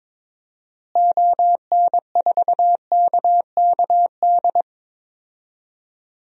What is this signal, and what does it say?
Morse code, heard as a keyed tone, about 22 wpm, 700 Hz. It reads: ON4KKD
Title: DXCC-100 Call Signs - 6 Characters - Repeated 2x - 22wpm